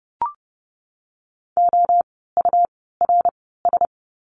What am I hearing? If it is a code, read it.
Morse code, heard as a keyed tone, about 30 wpm, spaced as for 17 wpm, 700 Hz. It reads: OURS
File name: Lesson 17f - Letter u - Words 30wpm